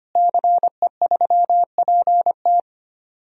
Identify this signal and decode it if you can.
Morse code, heard as a keyed tone, about 25 wpm, 700 Hz. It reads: CE3PT
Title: DXCC-100 Call Signs - 5 Characters - Repeated 2x - 25wpm